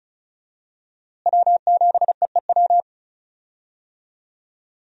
Morse code, heard as a keyed tone, about 36 wpm, 700 Hz. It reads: W7EEW
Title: USA Call Signs - 5 Characters - Repeated 2x - 35wpm